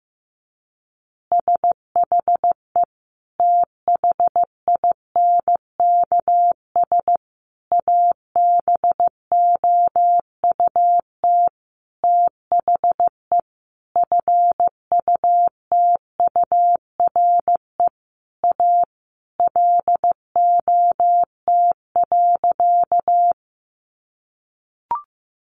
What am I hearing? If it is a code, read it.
Morse code, heard as a keyed tone, about 15 wpm, 700 Hz. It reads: SHE THINKS ABOUT THE FUTURE A LOT.
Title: Easy Phrases 15wpm